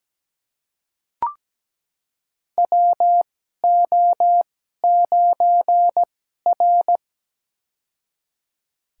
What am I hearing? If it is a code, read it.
Morse code, heard as a keyed tone, about 17 wpm, spaced as for 12 wpm, 700 Hz. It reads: WO9R
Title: USA Call Signs - 4 Characters - 2x Character Spacing - 17wpm